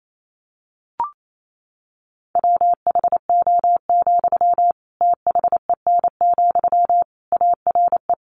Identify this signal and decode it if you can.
Morse code, heard as a keyed tone, about 28 wpm, 700 Hz. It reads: WHO, THEN, ARE
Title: Sentences from Top 100 Words 28wpm